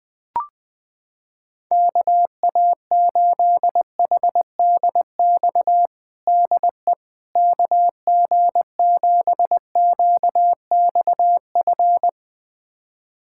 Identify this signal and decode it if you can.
Morse code, heard as a keyed tone, about 20 wpm, 700 Hz. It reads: KA8HDX DE KG7QXF
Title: USA Callsign DE Callsign - 6 Characters - 20wpm